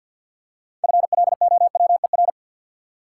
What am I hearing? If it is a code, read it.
Morse code, heard as a keyed tone, about 53 wpm, 700 Hz. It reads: FLOWER
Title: Six Letter Words - 50wpm